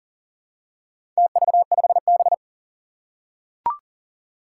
Morse code, heard as a keyed tone, about 40 wpm, 700 Hz. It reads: TV5B
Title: Four Challenging Letter-Number Combinations 40wpm